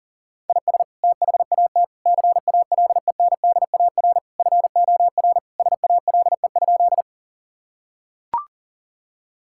Morse code, heard as a keyed tone, about 40 wpm, 700 Hz. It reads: IS THAT CALENDAR FOR SALE?
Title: Sentences from Top 1600 Words 40wpm